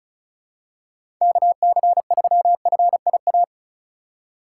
Morse code, heard as a keyed tone, about 34 wpm, 700 Hz. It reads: KC3FIA
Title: USA Call Signs - 6 Characters - 3x Character Spacing - 35wpm